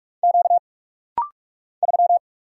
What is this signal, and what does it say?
Morse code, heard as a keyed tone, about 46 wpm, 700 Hz. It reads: Q 3
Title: Single Letter-Number - Rapid-Fire 45wpm